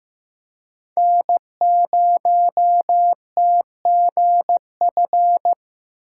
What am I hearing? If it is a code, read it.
Morse code, heard as a keyed tone, about 15 wpm, 700 Hz. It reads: N0TGF
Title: USA Call Signs - 5 Characters - 6x Character Spacing - 15wpm